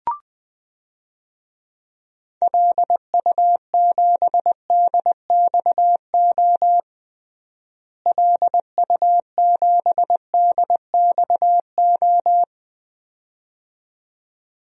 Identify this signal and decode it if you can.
Morse code, heard as a keyed tone, about 21 wpm, 700 Hz. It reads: LU7DXO LU7DXO
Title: DXCC-100 Call Signs - 6 Characters - Repeated 2x - 20wpm